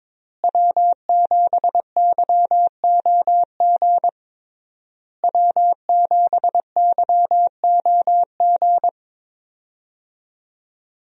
Morse code, heard as a keyed tone, about 22 wpm, 690 Hz. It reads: W7YOG W7YOG
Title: USA Call Signs - 5 Characters - Repeated 3x - 22wpm